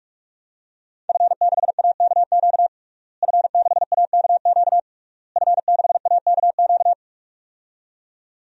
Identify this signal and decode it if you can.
Morse code, heard as a keyed tone, about 46 wpm, 700 Hz. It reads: F6AKQ F6AKQ F6AKQ
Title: DXCC-100 Call Signs - 5 Characters - Repeated 3x - 45wpm